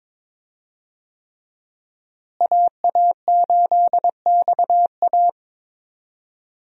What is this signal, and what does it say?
Morse code, heard as a keyed tone, about 22 wpm, 700 Hz. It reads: AA8XA
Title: USA Call Signs - 5 Characters - Repeated 3x - 22wpm